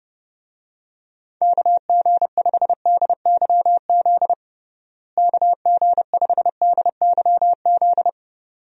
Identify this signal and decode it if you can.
Morse code, heard as a keyed tone, about 30 wpm, 700 Hz. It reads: KG5DYZ KG5DYZ
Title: USA Call Signs - 6 Characters - Repeated 2x - 30wpm